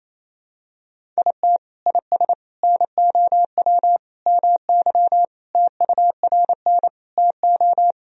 Morse code, heard as a keyed tone, about 28 wpm, 700 Hz. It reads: IT IS NOW MY TURN TO
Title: Sentences from Top 200 Words and Abbreviations 28wpm